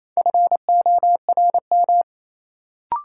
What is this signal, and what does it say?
Morse code, heard as a keyed tone, about 28 wpm, 700 Hz. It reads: FORM
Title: Sets of 4 Words - New Word Drill - 28wpm